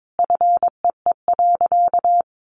Morse code, heard as a keyed tone, about 22 wpm, 700 Hz. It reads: FEE.
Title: Sentences from Top 1500 Words 22wpm